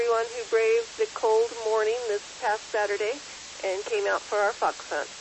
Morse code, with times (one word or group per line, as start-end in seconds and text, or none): none